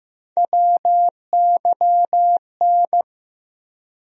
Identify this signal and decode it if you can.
Morse code, heard as a keyed tone, about 15 wpm, 700 Hz. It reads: WYN
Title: USA Call Sign Postfixes - 3 Letters 15wpm